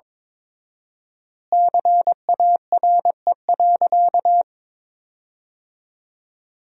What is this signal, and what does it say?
Morse code, heard as a keyed tone, about 22 wpm, 700 Hz. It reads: CARE.
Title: 4x Word-Spacing - Sentences from Top 300 Words - 22wpm